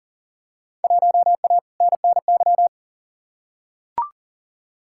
Morse code, heard as a keyed tone, about 40 wpm, 700 Hz. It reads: 1A NNY
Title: Field Day Exchanges - 40wpm